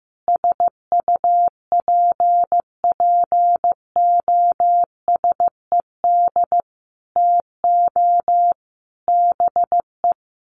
Morse code, heard as a keyed tone, about 15 wpm, 700 Hz. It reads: SUPPOSED TO BE
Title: Sets of 5 Words Encore 15wpm